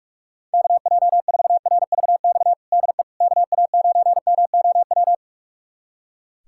Morse code, heard as a keyed tone, about 42 wpm, 690 Hz. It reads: KJ4RVX DE KA9MOW
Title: USA Callsign DE Callsign - 6 Characters - 45wpm